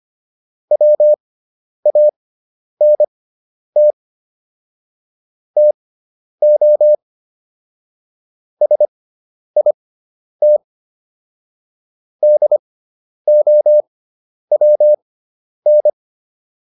Morse code, heard as a keyed tone, about 25 wpm, spaced as for 10 wpm, 600 Hz. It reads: WANT TO SIT DOWN